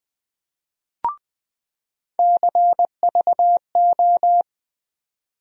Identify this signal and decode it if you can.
Morse code, heard as a keyed tone, about 20 wpm, 700 Hz. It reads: CVO
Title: USA Call Sign Postfixes - 3 Letters 20wpm